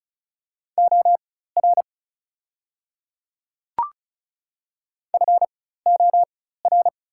Morse code, heard as a keyed tone, about 36 wpm, spaced as for 16 wpm, 700 Hz. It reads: OR FOR
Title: 4x Character Spacing - Top 100 Words - 35wpm